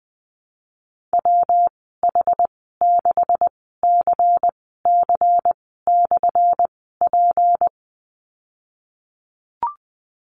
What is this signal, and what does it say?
Morse code, heard as a keyed tone, about 20 wpm, spaced as for 15 wpm, 700 Hz. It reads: WH6CC/P
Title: 179 Lesson 29g - Slash - Call Signs 20wpm